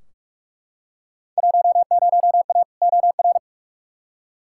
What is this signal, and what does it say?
Morse code, heard as a keyed tone, about 43 wpm, 700 Hz. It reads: 10A OR
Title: Field Day Exchanges - 45wpm